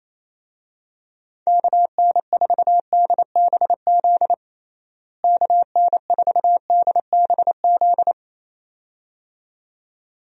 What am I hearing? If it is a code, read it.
Morse code, heard as a keyed tone, about 28 wpm, 700 Hz. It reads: KN4DBZ KN4DBZ
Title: USA Call Signs - 6 Characters - Repeated 2x - 28wpm